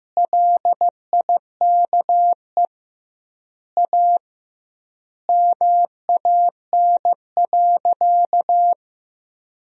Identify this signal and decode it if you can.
Morse code, heard as a keyed tone, about 15 wpm, 700 Hz. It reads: LIKE A MAN.
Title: 2x Word-Spacing - Sentences from Top 300 Words - 15wpm